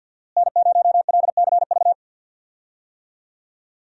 Morse code, heard as a keyed tone, about 52 wpm, 700 Hz. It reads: N0LCV
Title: USA Call Signs - 5 Characters - Repeated 3x - 50wpm